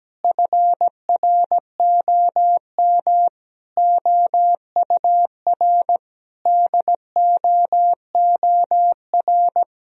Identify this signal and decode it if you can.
Morse code, heard as a keyed tone, about 17 wpm, 700 Hz. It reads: FROM OUR DOOR
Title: Sentences from Top 1000 Words - Review Part 1 - 17wpm